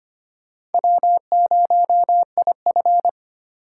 Morse code, heard as a keyed tone, about 32 wpm, 700 Hz. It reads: W0IF